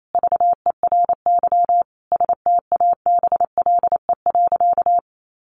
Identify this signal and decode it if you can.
Morse code, heard as a keyed tone, about 28 wpm, 700 Hz. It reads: VERY STABLE.
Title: Sentences from Top 1500 Words 28wpm